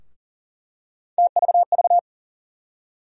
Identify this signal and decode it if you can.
Morse code, heard as a keyed tone, about 40 wpm, 700 Hz. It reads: TVV
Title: USA Call Sign Postfixes - 3 Letters 40wpm